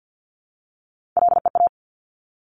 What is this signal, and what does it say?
Morse code, heard as a keyed tone, about 50 wpm, 700 Hz. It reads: LEA
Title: Top Three Letter Combinations in Words 50wpm